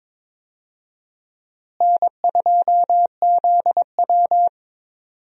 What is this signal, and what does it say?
Morse code, heard as a keyed tone, about 21 wpm, 700 Hz. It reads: N2ZW
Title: USA Call Signs - 4 Characters - Repeated 2x - 22wpm